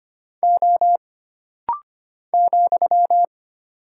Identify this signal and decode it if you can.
Morse code, heard as a keyed tone, about 25 wpm, 700 Hz. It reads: O ,
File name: Single Character (Letter:Number:Punctuation:Prosign) - Rapid-Fire - 25wpm